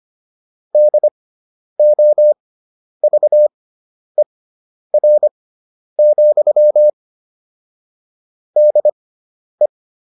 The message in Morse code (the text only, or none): DOVER, DE